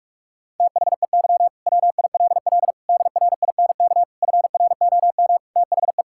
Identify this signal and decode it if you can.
Morse code, heard as a keyed tone, about 42 wpm, 700 Hz. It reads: THEY WILL DRINK FROM THE
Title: Sentences from Top 1600 Words 45wpm